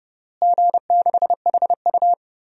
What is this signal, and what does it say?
Morse code, heard as a keyed tone, about 30 wpm, 700 Hz. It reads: G6HU
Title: DXCC-100 Call Signs - 4 Characters - Repeated 3x - 30wpm